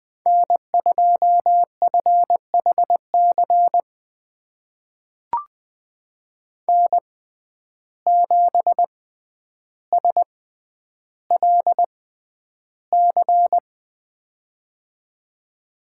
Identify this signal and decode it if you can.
Morse code, heard as a keyed tone, about 20 wpm, spaced as for 10 wpm, 700 Hz. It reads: N2FHC N7SLC
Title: USA Call Signs - 5 Characters - 6x Character Spacing - 20wpm